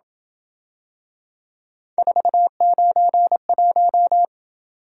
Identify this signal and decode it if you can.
Morse code, heard as a keyed tone, about 27 wpm, 700 Hz. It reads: 491